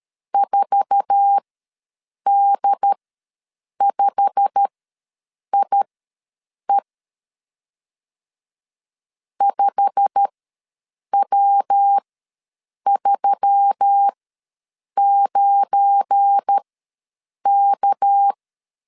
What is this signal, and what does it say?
Morse code, heard as a keyed tone, about 13 wpm, spaced as for 7 wpm, 800 Hz. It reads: 4D5IE 5W39K
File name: LEZIONI / 8 YJ45 / 8 YJ45 ws 12wpm 3146